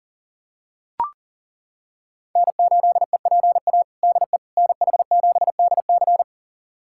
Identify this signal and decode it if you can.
Morse code, heard as a keyed tone, about 40 wpm, 700 Hz. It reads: N8EPA DE NH7DC
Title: USA Callsign DE Callsign - 5 Characters - 40wpm